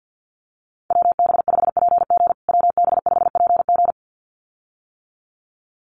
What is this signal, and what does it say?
Morse code, heard as a keyed tone, about 50 wpm, 700 Hz. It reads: WB5PG WB5PG
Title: USA Call Signs Repeated 2x 50wpm